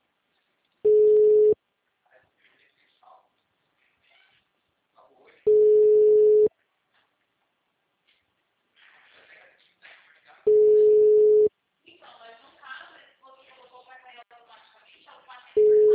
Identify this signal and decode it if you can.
Ringback tone (Europe, 425 Hz), with speech.
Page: Valinhos - SP